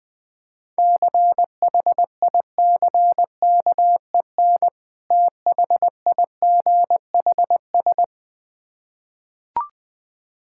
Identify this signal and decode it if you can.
Morse code, heard as a keyed tone, about 21 wpm, 690 Hz. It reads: CHICKEN THIGHS
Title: Difficult Words with Hints - 20wpm